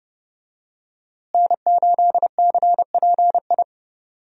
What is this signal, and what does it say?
Morse code, heard as a keyed tone, about 30 wpm, 700 Hz. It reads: N8CPI
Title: USA Call Signs - 5 Characters - Repeated 2x - 30wpm